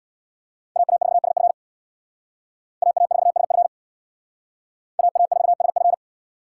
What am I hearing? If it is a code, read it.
Morse code, heard as a keyed tone, about 60 wpm, 700 Hz. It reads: AA4SF AA4SF AA4SF
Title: Speed Racing - USA 2x2 Call Signs 50wpm